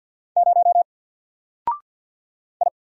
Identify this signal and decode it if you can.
Morse code, heard as a keyed tone, about 48 wpm, 700 Hz. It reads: 0 I
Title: Single Character (Letter:Number:Punctuation:Prosign) 50wpm